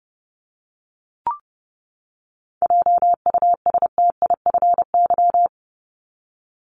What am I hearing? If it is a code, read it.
Morse code, heard as a keyed tone, about 30 wpm, 700 Hz. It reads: JUSTIFY
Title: Top 1751-2000 Words 30wpm